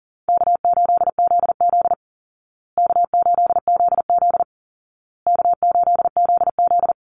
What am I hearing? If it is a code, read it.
Morse code, heard as a keyed tone, about 40 wpm, 700 Hz. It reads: K8ZZ K8ZZ K8ZZ
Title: Easy Call Signs Repeated 3x - 40wpm